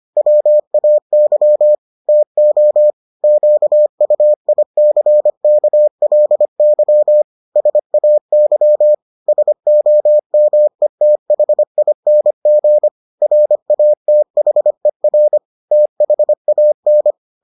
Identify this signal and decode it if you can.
Morse code, heard as a keyed tone, about 25 wpm, 600 Hz. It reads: WAY TO QUICKLY SAY SOMETHING RATHER THAN